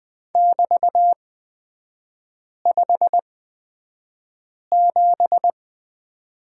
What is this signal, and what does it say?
Morse code, heard as a keyed tone, about 20 wpm, spaced as for 5 wpm, 700 Hz. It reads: <BT>57
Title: 236 Lesson 38e - Letter q - Three Characters 20wpm